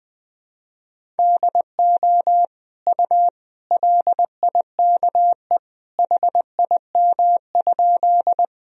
Morse code, heard as a keyed tone, about 20 wpm, 700 Hz. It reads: DO U LIKE HIM?
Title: Sentences from Top 100 Words and Abbreviations 20wpm